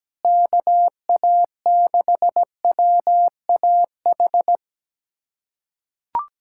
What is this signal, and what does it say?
Morse code, heard as a keyed tone, about 17 wpm, 700 Hz. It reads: KA6WAH
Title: USA Callsign DE Callsign - 6 Characters - 17wpm